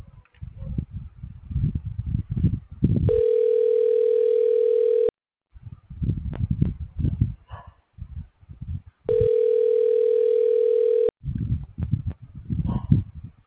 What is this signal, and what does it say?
Ringback tone (North America), 440 + 480 Hz.